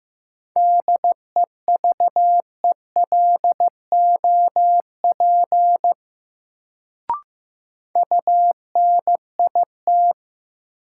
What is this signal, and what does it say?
Morse code, heard as a keyed tone, about 15 wpm, 700 Hz. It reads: DEVELOP UNIT
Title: Sentences from Top 1000 Words - Review Part 2 - New Word Drill - 15wpm